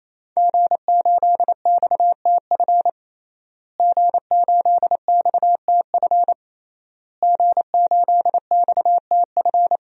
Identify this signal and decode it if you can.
Morse code, heard as a keyed tone, about 27 wpm, 700 Hz. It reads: G8XTF G8XTF G8XTF
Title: DXCC-100 Call Signs - 5 Characters - Repeated 3x - 28wpm